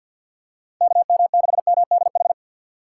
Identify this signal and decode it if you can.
Morse code, heard as a keyed tone, about 52 wpm, 700 Hz. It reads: KM6MDH
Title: USA Call Signs - 6 Characters - 4x Character Spacing - 50wpm